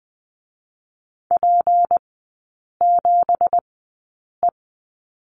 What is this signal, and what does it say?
Morse code, heard as a keyed tone, about 20 wpm, spaced as for 8 wpm, 700 Hz. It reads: P7E
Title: 166 Lesson 27e - Number 7 - Three Characters 20wpm